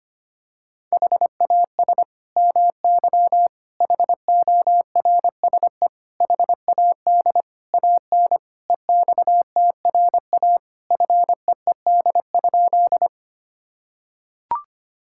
Morse code, heard as a keyed tone, about 25 wpm, 700 Hz. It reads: HAS MY HORSE HAD AN EXTRA FEED?
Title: Sentences from Top 1600 Words 25wpm